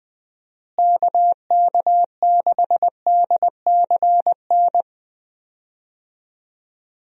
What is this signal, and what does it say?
Morse code, heard as a keyed tone, about 20 wpm, 700 Hz. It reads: KK6DCN
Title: USA Call Signs - 6 Characters - Repeated 3x - 20wpm